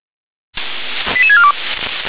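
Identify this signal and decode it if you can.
Five-tone selective call (ZVEI), heard as quick runs of five tones, tones of about 70 ms. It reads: 88642
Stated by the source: ZVEI five-tone call received by SDR.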